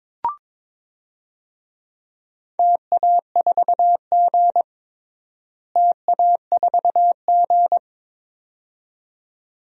Morse code, heard as a keyed tone, about 22 wpm, 700 Hz. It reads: TA4G TA4G
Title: DXCC-100 Call Signs - 4 Characters - Repeated 2x - 22wpm